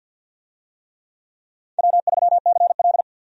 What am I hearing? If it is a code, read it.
Morse code, heard as a keyed tone, about 49 wpm, 700 Hz. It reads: W3CL
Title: USA Call Signs - 4 Characters - 50wpm